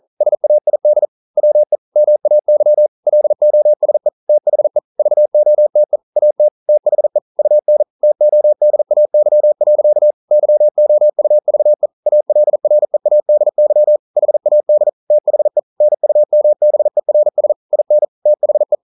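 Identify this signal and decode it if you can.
Morse code, heard as a keyed tone, about 42 wpm, 600 Hz. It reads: SAID WE MAY LOSE THE VOTE AT THE UN TODAY. YOUVE ALREADY HAD THE NUMBERS IN THE
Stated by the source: Morse code practice at 41 wpm, from CW med Gnister.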